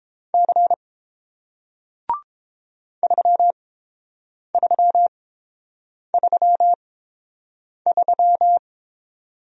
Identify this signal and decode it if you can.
Morse code, heard as a keyed tone, about 30 wpm, spaced as for 7 wpm, 700 Hz. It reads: C 3333